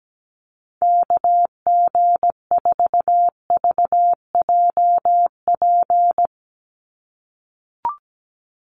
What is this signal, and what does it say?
Morse code, heard as a keyed tone, about 17 wpm, 700 Hz. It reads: KG4VJP
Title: USA Call Signs 17wpm